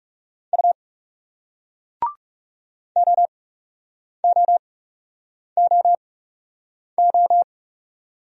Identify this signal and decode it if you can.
Morse code, heard as a keyed tone, about 40 wpm, 700 Hz. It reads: U O O O O